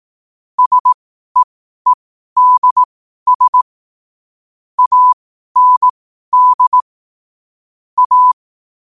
Morse code, heard as a keyed tone, about 18 wpm, spaced as for 12 wpm, 1000 Hz. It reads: SEEDS AND A
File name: qotdfiles / 12wpm